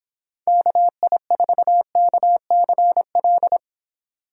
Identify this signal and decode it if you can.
Morse code, heard as a keyed tone, about 25 wpm, 700 Hz. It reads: KI4KCL
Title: Speed Racing - USA Call Signs 22wpm